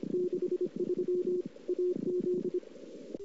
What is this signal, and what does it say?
Morse code, heard as a keyed tone, about 25 wpm, 360 Hz. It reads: 63 AZ
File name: CW / 2008-ss-cw-k5tr-op-n5rz -00:00